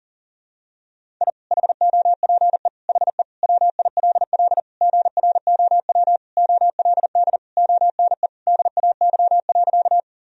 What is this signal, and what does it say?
Morse code, heard as a keyed tone, about 40 wpm, 700 Hz. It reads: I HOPE HE WILL GROW OLD ONE DAY.
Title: Sentences from Top 1000 Words - Review Part 1 - 40wpm